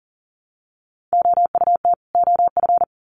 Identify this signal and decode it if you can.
Morse code, heard as a keyed tone, about 40 wpm, 700 Hz. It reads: OUT OF
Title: Sets of 2 Words 40wpm